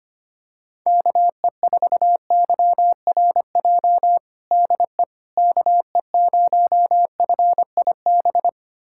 Morse code, heard as a keyed tone, about 25 wpm, 700 Hz. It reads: KE4YRJ DE KE0FIB
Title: USA Callsign DE Callsign - 6 Characters - 25wpm